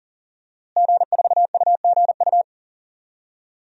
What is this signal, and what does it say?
Morse code, heard as a keyed tone, about 40 wpm, 700 Hz. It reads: G4UGU
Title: DXCC-100 Call Signs - 5 Characters - Repeated 2x - 40wpm